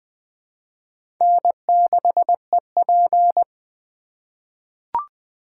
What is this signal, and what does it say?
Morse code, heard as a keyed tone, about 20 wpm, 700 Hz. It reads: N6EP
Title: USA Call Signs - 4 Characters - 3x Character Spacing - 20wpm